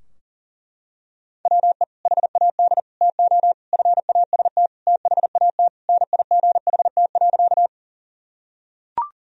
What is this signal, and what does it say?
Morse code, heard as a keyed tone, about 40 wpm, 700 Hz. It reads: WE HAD TO FAST THAT NIGHT.
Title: Sentences from Top 1000 Words - Review Part 2 - 40wpm